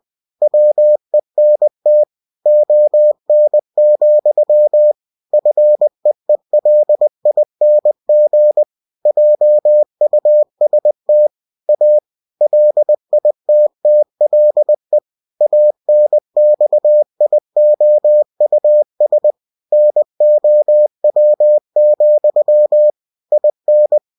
Morse code, heard as a keyed tone, about 20 wpm, 600 Hz. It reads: WENT ON, FEELING JUST A LITTLE ANXIOUS NOW, IN
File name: cw-books / winnie-the-pooh / winnie-20wpm-page-0023